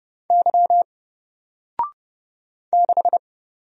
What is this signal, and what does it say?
Morse code, heard as a keyed tone, about 31 wpm, 700 Hz. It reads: Y 6
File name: Single-Letter-Number-30wpm